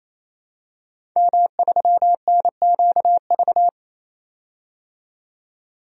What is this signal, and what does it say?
Morse code, heard as a keyed tone, about 28 wpm, 700 Hz. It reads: M3NQV
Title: DXCC-100 Call Signs - 5 Characters - 28wpm